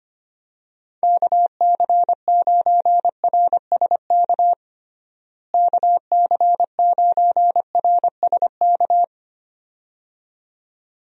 Morse code, heard as a keyed tone, about 25 wpm, 700 Hz. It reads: KC9RSK KC9RSK
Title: USA Call Signs - 6 Characters - Repeated 2x - 25wpm